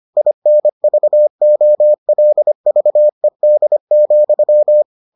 Morse code, heard as a keyed tone, about 25 wpm, 600 Hz. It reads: INVOLVED,